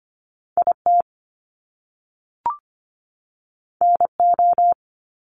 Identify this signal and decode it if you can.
Morse code, heard as a keyed tone, about 25 wpm, 700 Hz. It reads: IT NO
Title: Top Two Letter Combinations in Words 25wpm